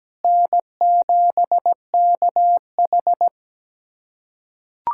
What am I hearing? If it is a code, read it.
Morse code, heard as a keyed tone, about 17 wpm, 700 Hz. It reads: N7KH
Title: USA Callsign DE Callsign - 4 Characters - 17wpm